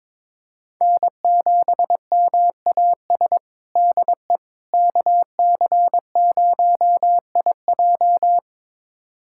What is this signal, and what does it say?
Morse code, heard as a keyed tone, about 22 wpm, 700 Hz. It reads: N7MAS DE KC0IJ
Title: USA Callsign DE Callsign - 5 Characters - 22wpm